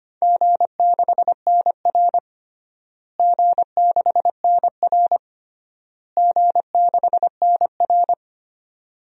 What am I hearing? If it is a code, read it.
Morse code, heard as a keyed tone, about 25 wpm, 700 Hz. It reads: G6NR G6NR G6NR
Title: DXCC-100 Call Signs - 4 Characters - Repeated 3x - 25wpm